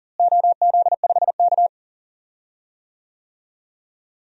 Morse code, heard as a keyed tone, about 40 wpm, 700 Hz. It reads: OZ5K
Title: DXCC-100 Call Signs - 4 Characters - Repeated 3x - 40wpm